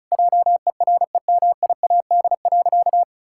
Morse code, heard as a keyed tone, about 36 wpm, 700 Hz. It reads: JEREMIAD.
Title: Pangram Challenge v2 - 35wpm